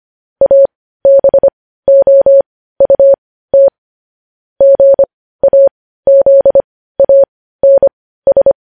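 Morse code, heard as a keyed tone, about 25 wpm, spaced as for 15 wpm, 550 Hz. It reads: ABOUT GAZANS
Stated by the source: News headlines sent in Morse code at 15 words per minute, preceded by vvv.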